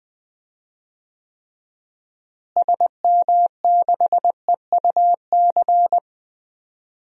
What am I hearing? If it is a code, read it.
Morse code, heard as a keyed tone, about 20 wpm, 700 Hz. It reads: SM6EUC
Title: DXCC-100 Call Signs - 6 Characters - Repeated 2x - 20wpm